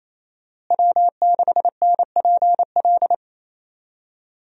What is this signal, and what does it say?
Morse code, heard as a keyed tone, about 28 wpm, 700 Hz. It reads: W6NPL
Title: USA Call Signs - 5 Characters - Repeated 3x - 28wpm